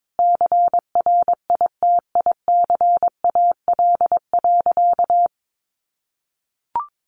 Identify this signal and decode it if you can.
Morse code, heard as a keyed tone, about 22 wpm, 700 Hz. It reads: CRITICAL.
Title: Sentences from Top 1100 Words 22wpm